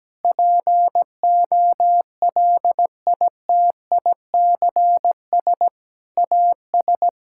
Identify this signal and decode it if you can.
Morse code, heard as a keyed tone, about 17 wpm, 700 Hz. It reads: POLITICS AS
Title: Sentences from Top 1000 Words - Review Part 6 - 17wpm